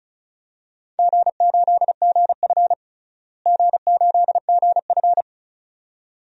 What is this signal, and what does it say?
Morse code, heard as a keyed tone, about 34 wpm, 700 Hz. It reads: G8GF G8GF
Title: DXCC-100 Call Signs - 4 Characters - Repeated 2x - 35wpm